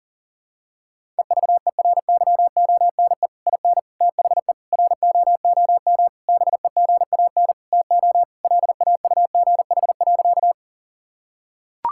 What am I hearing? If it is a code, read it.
Morse code, heard as a keyed tone, about 40 wpm, 700 Hz. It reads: EVERYONE IN THE ROOM BEGAN TO LAUGH.
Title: Sentences from Top 1000 Words - Review Part 5 - 40wpm